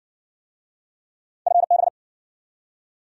Morse code, heard as a keyed tone, about 60 wpm, 700 Hz. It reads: FB